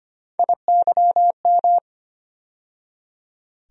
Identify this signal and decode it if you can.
Morse code, heard as a keyed tone, about 25 wpm, 700 Hz. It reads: IYM
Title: USA Call Sign Postfixes - 3 Letters 25wpm